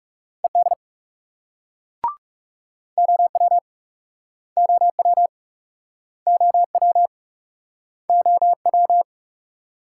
Morse code, heard as a keyed tone, about 39 wpm, 700 Hz. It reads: ED OW OW OW OW